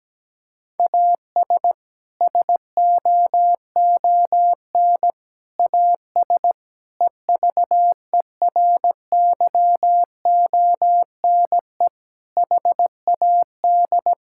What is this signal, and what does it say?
Morse code, heard as a keyed tone, about 17 wpm, 700 Hz. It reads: AS SOON AS EVERYONE HAD
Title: Sentences from Top 1000 Words - Review Part 5 - 17wpm